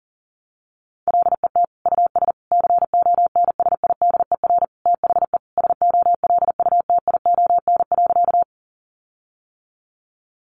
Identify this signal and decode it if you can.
Morse code, heard as a keyed tone, about 40 wpm, 700 Hz. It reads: LET US CONSIDER THE SOLUTION.
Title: Sentences from Top 1300 Words 40wpm